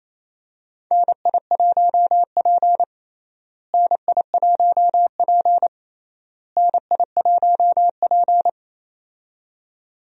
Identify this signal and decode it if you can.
Morse code, heard as a keyed tone, about 28 wpm, 700 Hz. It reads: NI1P NI1P NI1P
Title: USA Call Signs - 4 Characters - Repeated 3x - 28wpm